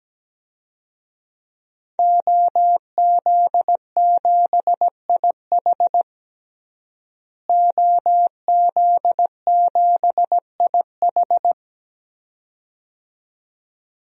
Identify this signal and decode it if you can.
Morse code, heard as a keyed tone, about 17 wpm, 700 Hz. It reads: OZ7IH OZ7IH
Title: DXCC-100 Call Signs - 5 Characters - Repeated 2x - 17wpm